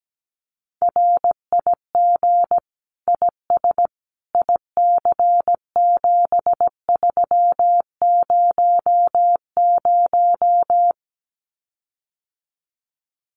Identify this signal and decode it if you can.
Morse code, heard as a keyed tone, about 17 wpm, 700 Hz. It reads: RIG IS IC7300
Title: Common QSO Phrases Encore 17wpm